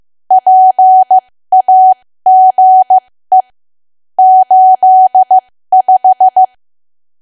Morse code, heard as a keyed tone, about 15 wpm, spaced as for 13 wpm, 750 Hz. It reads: PAGE 85
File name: cwa / W1AW-Level2 / 13WPM